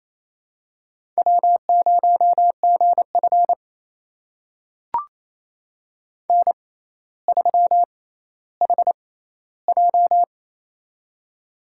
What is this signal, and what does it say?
Morse code, heard as a keyed tone, about 28 wpm, spaced as for 14 wpm, 700 Hz. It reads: W0GF N3HJ
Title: USA Call Signs - 4 Characters - 6x Character Spacing - 28wpm